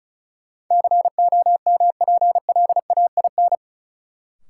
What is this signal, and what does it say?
Morse code, heard as a keyed tone, about 36 wpm, 700 Hz. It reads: COMPLAIN
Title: Sentences from Top 1300 Words - New Word Drill - 35wpm